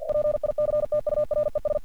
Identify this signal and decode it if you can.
Morse code, heard as a keyed tone, about 48 wpm, 600 Hz. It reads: PICTURES
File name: archived / CRSnet